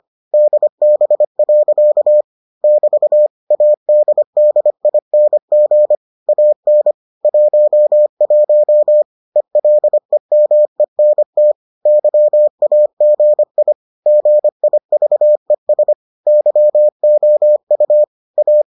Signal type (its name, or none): Morse code